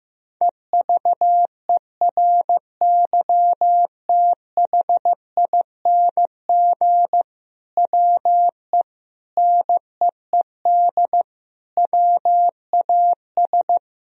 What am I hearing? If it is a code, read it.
Morse code, heard as a keyed tone, about 15 wpm, 700 Hz. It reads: EVERYTHING WE NEED WAS
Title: Sentences from Top 1000 Words - Review Part 1 - 15wpm